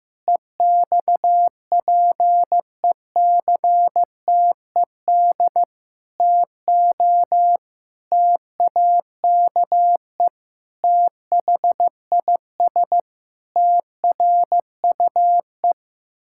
Morse code, heard as a keyed tone, about 16 wpm, 700 Hz. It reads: EXPECTED TO TAKE THIS TRUE
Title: Pangram Challenge v2 - 15wpm